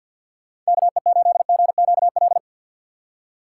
Morse code, heard as a keyed tone, about 59 wpm, 700 Hz. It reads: KE8GQL
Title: Speed Racing - USA Call Signs 50wpm